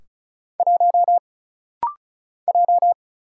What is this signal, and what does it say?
Morse code, heard as a keyed tone, about 36 wpm, 700 Hz. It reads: 1 J
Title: Single Letter-Number - Rapid-Fire 35wpm